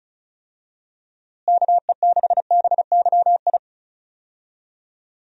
Morse code, heard as a keyed tone, about 36 wpm, 700 Hz. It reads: KE6BYI
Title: USA Call Signs - 6 Characters - Repeated 3x - 35wpm